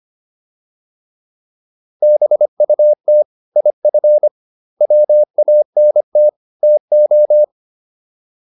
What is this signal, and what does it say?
Morse code, heard as a keyed tone, about 25 wpm, 600 Hz. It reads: BUT IF WANT TO